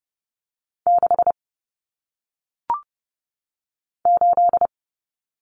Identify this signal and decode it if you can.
Morse code, heard as a keyed tone, about 30 wpm, 700 Hz. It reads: 6 8
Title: Numbers - 1-Digit 30wpm